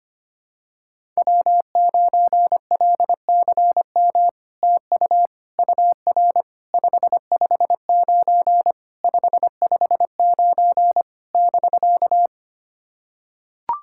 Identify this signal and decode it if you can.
Morse code, heard as a keyed tone, about 25 wpm, 700 Hz. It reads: W9LCM TU UR 559 559 <BK>
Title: SOTA QSO Practice 25wpm